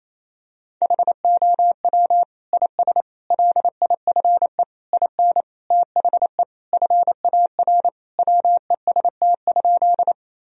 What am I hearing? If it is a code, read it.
Morse code, heard as a keyed tone, about 28 wpm, 700 Hz. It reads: HOW IS LIFE IN THE FAR WEST?
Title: Sentences from Top 1000 Words - Review Part 2 - 28wpm